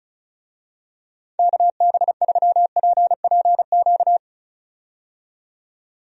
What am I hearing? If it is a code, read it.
Morse code, heard as a keyed tone, about 35 wpm, 700 Hz. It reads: KB3PPQ
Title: USA Call Signs - 6 Characters - 35wpm